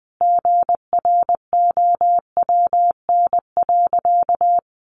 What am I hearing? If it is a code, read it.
Morse code, heard as a keyed tone, about 20 wpm, 700 Hz. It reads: GROWN.
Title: Easy Phrases 20wpm